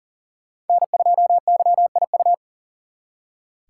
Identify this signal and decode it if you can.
Morse code, heard as a keyed tone, about 52 wpm, 700 Hz. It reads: N2YIU